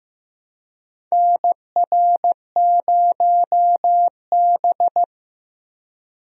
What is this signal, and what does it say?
Morse code, heard as a keyed tone, about 15 wpm, 700 Hz. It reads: NR0B
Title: USA Call Signs - 4 Characters - Repeated 3x - 15wpm